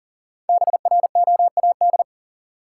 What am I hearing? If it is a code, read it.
Morse code, heard as a keyed tone, about 40 wpm, 700 Hz. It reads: BROAD
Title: Five Letter Words - 40wpm